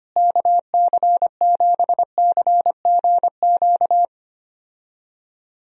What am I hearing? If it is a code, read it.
Morse code, heard as a keyed tone, about 25 wpm, 700 Hz. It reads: KC7CGQ
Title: USA Call Signs - 6 Characters - Repeated 2x - 25wpm